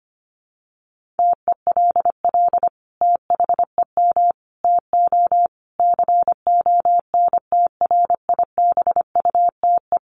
Morse code, heard as a keyed tone, about 25 wpm, 700 Hz. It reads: TELL THEM TO CONTRIBUTE
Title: Sentences from Top 1400 Words 25wpm